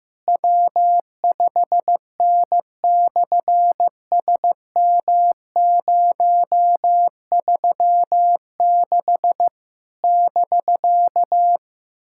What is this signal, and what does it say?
Morse code, heard as a keyed tone, about 15 wpm, 700 Hz. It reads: W5N/SM036 <BK>
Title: SOTA QSO Practice 15wpm